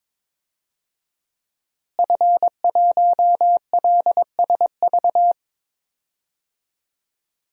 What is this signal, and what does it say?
Morse code, heard as a keyed tone, about 22 wpm, 700 Hz. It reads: F1LSV
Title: DXCC-100 Call Signs - 5 Characters - Repeated 2x - 22wpm